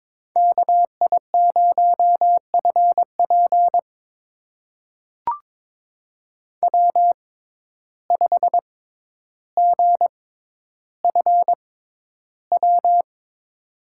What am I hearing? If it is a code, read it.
Morse code, heard as a keyed tone, about 21 wpm, spaced as for 11 wpm, 700 Hz. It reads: KI0FP W5GFW
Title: USA Call Signs - 5 Characters - 6x Character Spacing - 22wpm